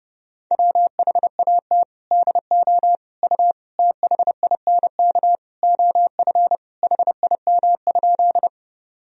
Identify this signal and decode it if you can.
Morse code, heard as a keyed tone, about 30 wpm, 700 Hz. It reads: WHAT DO U THINK OF HIM?
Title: Sentences from Top 200 Words and Abbreviations 30wpm